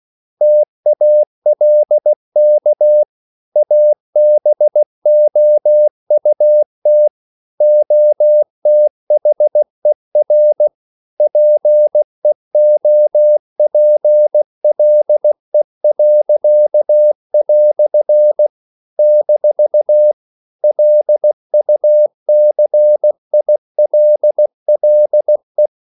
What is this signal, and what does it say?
Morse code, heard as a keyed tone, about 16 wpm, 600 Hz. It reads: TALK ABOUT OTHER PEOPLE." - LUCILLE